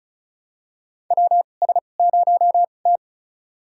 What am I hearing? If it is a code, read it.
Morse code, heard as a keyed tone, about 34 wpm, spaced as for 25 wpm, 700 Hz. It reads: WS0T
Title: USA Call Signs - 2x Character Spacing - 35wpm